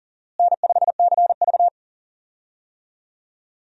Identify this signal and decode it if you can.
Morse code, heard as a keyed tone, about 40 wpm, 700 Hz. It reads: N5CV
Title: USA Call Signs - 4 Characters - Repeated 3x - 40wpm